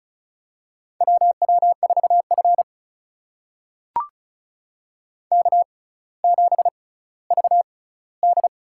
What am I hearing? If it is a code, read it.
Morse code, heard as a keyed tone, about 36 wpm, spaced as for 18 wpm, 700 Hz. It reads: WW4F K7VD
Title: USA Call Signs - 4 Characters - 6x Character Spacing - 35wpm